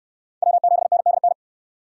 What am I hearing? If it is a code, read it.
Morse code, heard as a keyed tone, about 68 wpm, 700 Hz. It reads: W7NDN